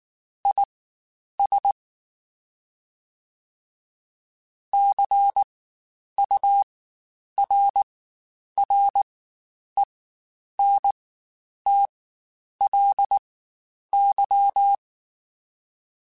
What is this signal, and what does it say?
Morse code, heard as a keyed tone, about 19 wpm, spaced as for 9 wpm, 800 Hz. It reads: IS CURRENTLY
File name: Spaced Characters / Spaced Characters-mp3s / Weather5 12 sp @18